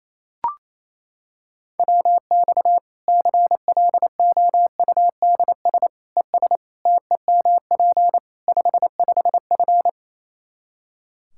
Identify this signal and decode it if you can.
Morse code, heard as a keyed tone, about 28 wpm, 700 Hz. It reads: WX CLOUDS ES TEMP 55F
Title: Practice QSO - Exchange 2 - Info - 28wpm